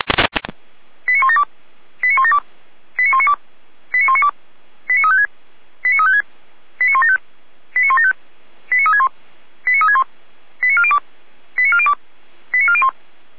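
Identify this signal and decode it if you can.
Five-tone selective call (ZVEI), heard as quick runs of five tones, tones of about 80 ms. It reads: 89172 89172 89192 89192 89356 89356 89175 89175 89361 89361 89402 89402 89401
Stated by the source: ZVEI five-tone call 89582 received by SDR, recorded by the Feuerwehr Orlamünde fire brigade (F85360).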